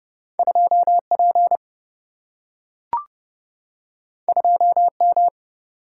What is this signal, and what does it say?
Morse code, heard as a keyed tone, about 30 wpm, 700 Hz. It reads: 2 P 2 M
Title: Single-Letter Single-Number 30wpm